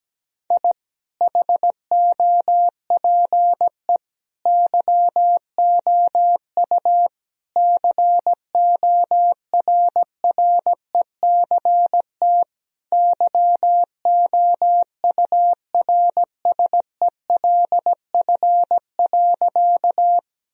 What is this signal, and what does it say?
Morse code, heard as a keyed tone, about 17 wpm, 700 Hz. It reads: I HOPE YOU CORRECT YOURSELF.
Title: Sentences from Top 1000 Words - Review Part 2 - 17wpm